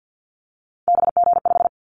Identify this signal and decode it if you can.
Morse code, heard as a keyed tone, about 50 wpm, 700 Hz. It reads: BG5
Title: Callsign Prefixes for Top 100 DXCC - 50wpm